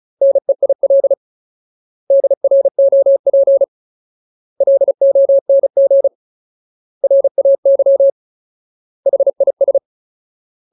Morse code, heard as a keyed tone, about 34 wpm, 550 Hz. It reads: NEIL DROP LONG RAY HIS
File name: Ultimate Word List / 35WPM Ultimate